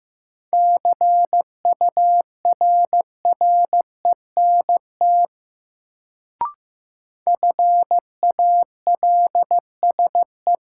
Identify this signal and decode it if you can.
Morse code, heard as a keyed tone, about 15 wpm, 700 Hz. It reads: CURRENT FALSE
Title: Sentences from Top 1600 Words - New Word Drill Alt - 15wpm